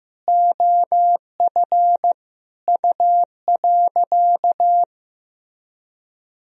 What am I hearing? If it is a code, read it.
Morse code, heard as a keyed tone, about 15 wpm, 700 Hz. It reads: OF U.
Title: Sentences from Top 200 Words and Abbreviations 15wpm